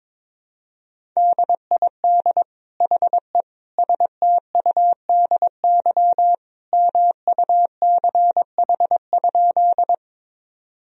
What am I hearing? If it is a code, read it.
Morse code, heard as a keyed tone, about 22 wpm, 700 Hz. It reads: DID HE STUDY MUCH?
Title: Sentences from Top 1000 Words - Review Part 1 - 22wpm